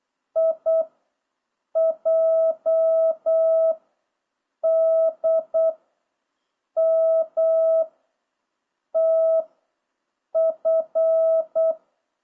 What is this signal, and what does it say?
Morse code, heard as a keyed tone, about 8 wpm, 660 Hz. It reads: I J D M T F